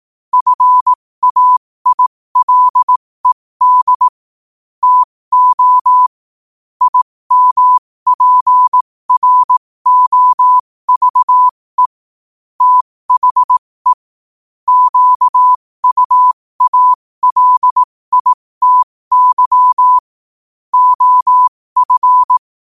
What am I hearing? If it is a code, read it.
Morse code, heard as a keyed tone, about 18 wpm, spaced as for 15 wpm, 1000 Hz. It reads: FAILED TO IMPROVE THE QUALITY OF
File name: qotdfiles / 15wpm